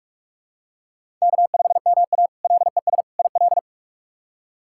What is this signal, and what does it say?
Morse code, heard as a keyed tone, about 46 wpm, 700 Hz. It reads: K5MA LES IL
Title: K1USN SST QSO Practice - 45wpm